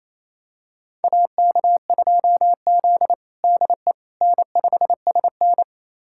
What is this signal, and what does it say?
Morse code, heard as a keyed tone, about 28 wpm, 700 Hz. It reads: AK2Z DE N5SN
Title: USA Callsign DE Callsign - 4 Characters - 28wpm